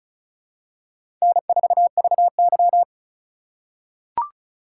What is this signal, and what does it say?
Morse code, heard as a keyed tone, about 36 wpm, 700 Hz. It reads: N4VY
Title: USA Call Signs - 4 Characters - 2x Character Spacing - 35wpm